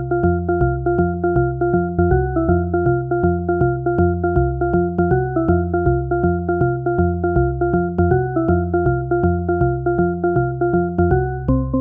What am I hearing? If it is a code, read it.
There is no code.